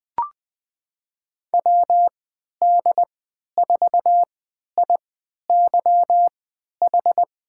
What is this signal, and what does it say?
Morse code, heard as a keyed tone, about 20 wpm, spaced as for 11 wpm, 700 Hz. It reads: WD4IYH
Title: USA Call Signs - 6 Characters - 3x Character Spacing - 20wpm